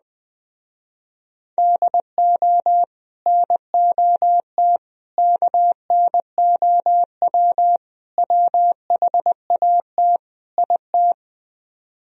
Morse code, heard as a keyed tone, about 20 wpm, 700 Hz. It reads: DO NOT KNOW WHAT IT
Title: Speed-Racing - Sets of 5 Words 17wpm